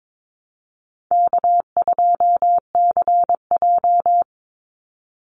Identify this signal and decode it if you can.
Morse code, heard as a keyed tone, about 22 wpm, 700 Hz. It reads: K2CJ
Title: Call Signs - 22wpm